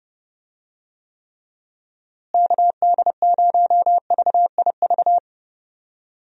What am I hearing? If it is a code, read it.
Morse code, heard as a keyed tone, about 30 wpm, 700 Hz. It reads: KD0VIV
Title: USA Call Signs - 6 Characters - Repeated 3x - 30wpm